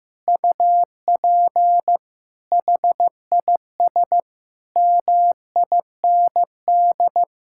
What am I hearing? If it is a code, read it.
Morse code, heard as a keyed tone, about 15 wpm, 700 Hz. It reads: UP HIS MIND
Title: Sentences from Top 1600 Words 15wpm